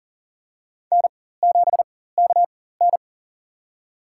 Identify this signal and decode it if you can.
Morse code, heard as a keyed tone, about 40 wpm, spaced as for 19 wpm, 700 Hz. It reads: N7KN
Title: USA Call Signs - 4 Characters - 4x Character Spacing - 40wpm